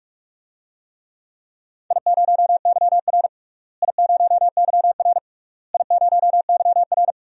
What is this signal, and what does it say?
Morse code, heard as a keyed tone, about 41 wpm, 700 Hz. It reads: I0YR I0YR I0YR
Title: DXCC-100 Call Signs - 4 Characters - Repeated 3x - 45wpm